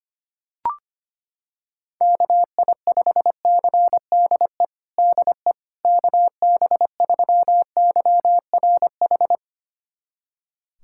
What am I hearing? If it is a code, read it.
Morse code, heard as a keyed tone, about 25 wpm, 700 Hz. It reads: KI5CDE DE KB3YRH
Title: USA Callsign DE Callsign 25wpm